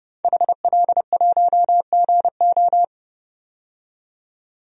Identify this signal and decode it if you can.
Morse code, heard as a keyed tone, about 30 wpm, 700 Hz. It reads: HL1GO
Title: DXCC-100 Call Signs - 5 Characters - 30wpm